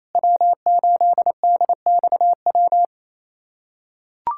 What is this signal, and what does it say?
Morse code, heard as a keyed tone, about 28 wpm, 700 Hz. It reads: W8DXW
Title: USA Callsign DE Callsign - 5 Characters - 28wpm